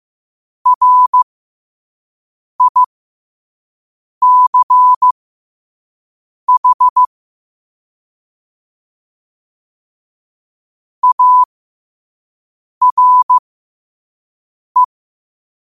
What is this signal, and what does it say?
Morse code, heard as a keyed tone, about 15 wpm, spaced as for 5 wpm, 1000 Hz. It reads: RICH ARE
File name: qotdfiles / 5wpm